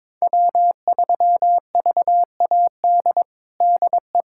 Morse code, heard as a keyed tone, about 22 wpm, 700 Hz. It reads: W3VAD DE
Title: USA Callsign DE Callsign - 5 Characters - 22wpm